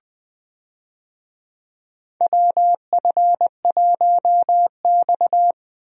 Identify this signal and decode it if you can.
Morse code, heard as a keyed tone, about 20 wpm, 700 Hz. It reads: WF1X
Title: USA Call Signs - 4 Characters - Repeated 2x - 20wpm